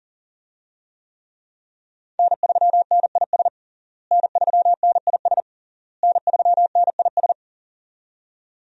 Morse code, heard as a keyed tone, about 40 wpm, 700 Hz. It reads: N3NIS N3NIS N3NIS
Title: USA Call Signs - 5 Characters - Repeated 3x - 40wpm